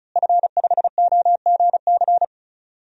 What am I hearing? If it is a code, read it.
Morse code, heard as a keyed tone, about 36 wpm, 700 Hz. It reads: F5OGC
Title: DXCC-100 Call Signs - 5 Characters - Repeated 2x - 35wpm